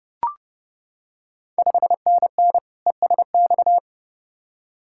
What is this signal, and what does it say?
Morse code, heard as a keyed tone, about 29 wpm, 700 Hz. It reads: 5NN ESX
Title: Ontario QSO Party Exchange 30wpm